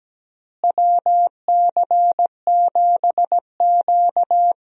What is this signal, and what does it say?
Morse code, heard as a keyed tone, about 17 wpm, 700 Hz. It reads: WC7Q
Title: USA Call Signs - 4 Characters - Repeated 2x - 17wpm